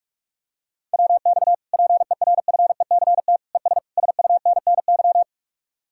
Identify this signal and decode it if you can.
Morse code, heard as a keyed tone, about 45 wpm, 700 Hz. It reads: WX PERFECT ES SUNNY